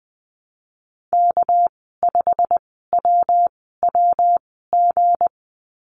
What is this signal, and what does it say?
Morse code, heard as a keyed tone, about 20 wpm, spaced as for 15 wpm, 700 Hz. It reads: K5WWG
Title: 219 Lesson 35g - Number 0 - Call Signs 20wpm